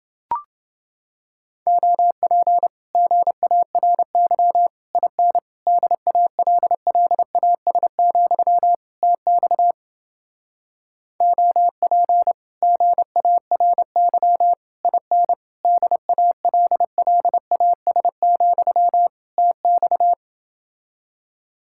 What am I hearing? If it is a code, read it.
Morse code, heard as a keyed tone, about 29 wpm, 700 Hz. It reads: OP GARY IN DALLAS, TX OP GARY IN DALLAS, TX